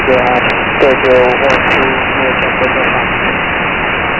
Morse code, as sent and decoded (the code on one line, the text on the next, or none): none